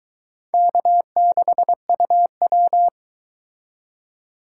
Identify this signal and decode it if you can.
Morse code, heard as a keyed tone, about 22 wpm, 700 Hz. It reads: K6UW